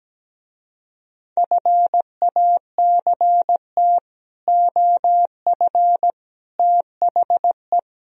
Morse code, heard as a keyed tone, about 17 wpm, 700 Hz. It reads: FACT OF THE
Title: Speed-Racing - Sets of 5 Words 17wpm